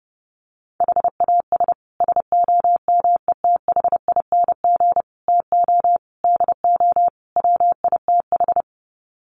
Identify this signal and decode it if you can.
Morse code, heard as a keyed tone, about 30 wpm, 700 Hz. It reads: HAS SOMETHING TO DO WITH
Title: Sets of 5 Words Encore 30wpm